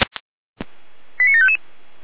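Five-tone selective call (ZVEI), heard as quick runs of five tones, tones of about 70 ms. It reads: 89755
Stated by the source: ZVEI five-tone call received by SDR.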